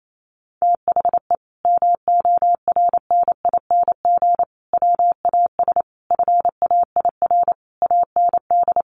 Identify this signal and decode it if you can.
Morse code, heard as a keyed tone, about 28 wpm, 700 Hz. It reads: THE MORNING WAS FAIR AND
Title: Sentences from Top 1300 Words 28wpm